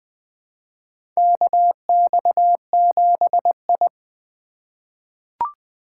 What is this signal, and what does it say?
Morse code, heard as a keyed tone, about 20 wpm, 700 Hz. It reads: KX7I
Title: USA Call Signs - 4 Characters - 4x Character Spacing - 20wpm